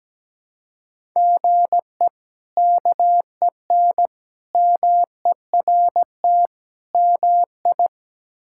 Morse code, heard as a keyed tone, about 17 wpm, 700 Hz. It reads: GE KEN MERT MI
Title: K1USN SST QSO Practice - 17wpm